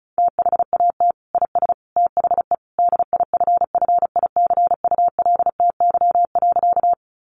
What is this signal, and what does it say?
Morse code, heard as a keyed tone, about 34 wpm, 700 Hz. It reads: THAT IS THE DIFFICULTY.
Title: Sentences from Top 1500 Words 35wpm